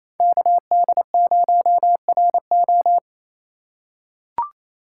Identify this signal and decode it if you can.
Morse code, heard as a keyed tone, about 28 wpm, 700 Hz. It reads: KD0RO
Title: USA Call Signs - 5 Characters - 2x Character Spacing - 28wpm